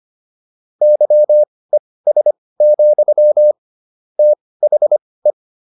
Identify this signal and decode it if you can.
Morse code, heard as a keyed tone, about 25 wpm, spaced as for 18 wpm, 600 Hz. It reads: YES, THE